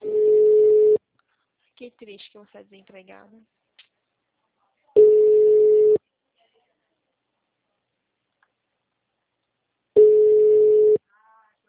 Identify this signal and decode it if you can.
Ringback tone (Europe, 425 Hz), with speech.